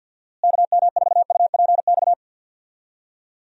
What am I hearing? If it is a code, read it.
Morse code, heard as a keyed tone, about 51 wpm, 700 Hz. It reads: KM4UPX
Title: USA Call Signs - 6 Characters - Repeated 2x - 50wpm